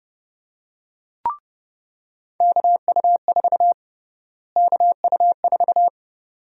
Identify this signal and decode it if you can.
Morse code, heard as a keyed tone, about 30 wpm, 700 Hz. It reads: KU4 KU4
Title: USA Call Sign Prefixes - Repeated 2x - 30wpm